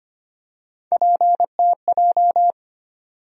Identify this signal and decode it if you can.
Morse code, heard as a keyed tone, about 25 wpm, 700 Hz. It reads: PTJ
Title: USA Call Sign Postfixes - 3 Letters 25wpm